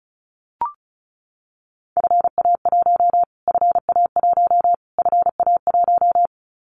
Morse code, heard as a keyed tone, about 34 wpm, 700 Hz. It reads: FA1 FA1 FA1
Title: Callsign Prefixes for Top 25 DXCC - Repeated 3x - 35wpm